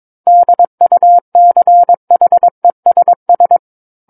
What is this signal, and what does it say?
Morse code, heard as a keyed tone, about 22 wpm, 700 Hz. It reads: DUCHESS
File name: 06-Chapter-06 22 WPM